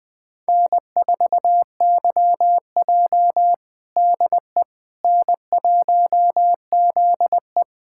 Morse code, heard as a keyed tone, about 20 wpm, 700 Hz. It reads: N4YJ DE N1ZE
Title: USA Callsign DE Callsign - 4 Characters - 20wpm